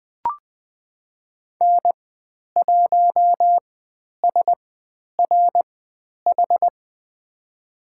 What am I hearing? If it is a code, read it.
Morse code, heard as a keyed tone, about 20 wpm, spaced as for 10 wpm, 700 Hz. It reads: N1SRH
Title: 102 Lesson 17g - Letter u - Call Signs 20wpm